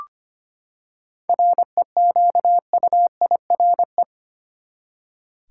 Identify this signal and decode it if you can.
Morse code, heard as a keyed tone, about 25 wpm, 700 Hz. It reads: REQUIRE
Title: Seven Letter Words - 25wpm